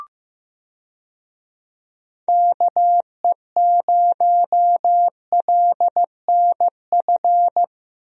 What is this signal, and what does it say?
Morse code, heard as a keyed tone, about 15 wpm, 700 Hz. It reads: KE0LNF
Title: USA Call Signs - 6 Characters - Repeated 3x - 15wpm